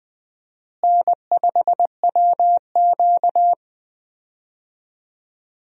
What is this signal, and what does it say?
Morse code, heard as a keyed tone, about 20 wpm, 700 Hz. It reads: N5WQ
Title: USA Call Signs - 4 Characters - Repeated 2x - 20wpm